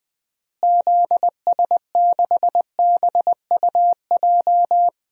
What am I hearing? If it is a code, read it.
Morse code, heard as a keyed tone, about 20 wpm, 700 Hz. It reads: ZS6BUJ
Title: DXCC-100 Call Signs - 6 Characters - Repeated 3x - 20wpm